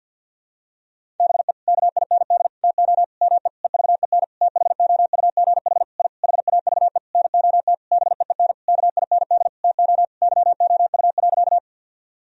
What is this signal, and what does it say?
Morse code, heard as a keyed tone, about 46 wpm, 700 Hz. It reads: BE KIND TO ME EVEN THOUGH I HAVE NOT BEEN KIND TO YOU.
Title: Sentences from Top 1000 Words - Review Part 2 - 50wpm